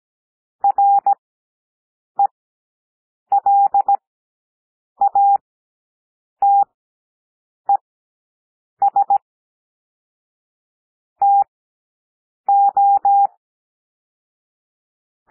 Morse code, heard as a keyed tone, about 17 wpm, spaced as for 7 wpm, 800 Hz. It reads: RELATES TO
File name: SOUND PLAIN 17 wpm